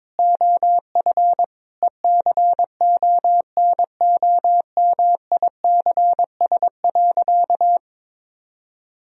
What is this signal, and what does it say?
Morse code, heard as a keyed tone, about 22 wpm, 700 Hz. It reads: OF ECONOMICS.
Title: Sentences from Top 1000 Words - Review Part 3 - 22wpm